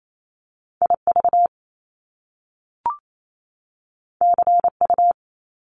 Morse code, heard as a keyed tone, about 28 wpm, 700 Hz. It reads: IV CU
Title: Top Two Letter Combinations in Words 28wpm